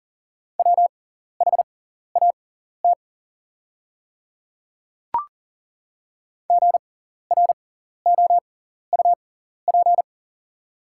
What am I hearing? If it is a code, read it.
Morse code, heard as a keyed tone, about 41 wpm, spaced as for 14 wpm, 690 Hz. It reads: WHAT GROUP
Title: 6x Character Spacing - Top 300 Words - 40wpm